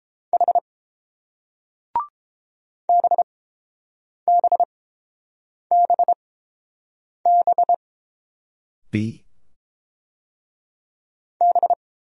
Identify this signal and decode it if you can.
Morse code, heard as a keyed tone, about 30 wpm, spaced as for 7 wpm, 690 Hz. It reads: H BBBB B